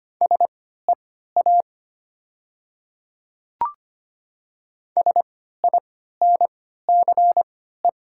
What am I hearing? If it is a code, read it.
Morse code, heard as a keyed tone, about 25 wpm, spaced as for 14 wpm, 700 Hz. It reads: SEA SINCE
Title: 3x Character Spacing - Top 300 Words - 25wpm